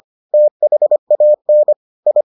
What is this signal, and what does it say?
Morse code, heard as a keyed tone, about 25 wpm, 600 Hz. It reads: THAN I